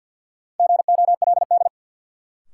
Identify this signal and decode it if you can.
Morse code, heard as a keyed tone, about 52 wpm, 700 Hz. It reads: GOLD
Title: Sentences from Top 1100 Words - New Word Drill Alt - 50wpm